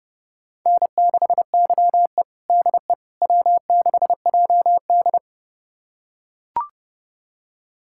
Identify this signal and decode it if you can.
Morse code, heard as a keyed tone, about 30 wpm, 700 Hz. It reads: N6YE DE W6JD
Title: USA Callsign DE Callsign - 4 Characters - 30wpm